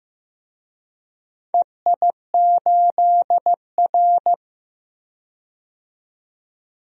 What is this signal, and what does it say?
Morse code, heard as a keyed tone, about 15 wpm, 700 Hz. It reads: EI8R
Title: DXCC-100 Call Signs - 4 Characters - 15wpm